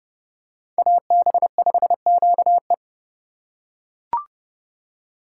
Morse code, heard as a keyed tone, about 30 wpm, 700 Hz. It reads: AB5QE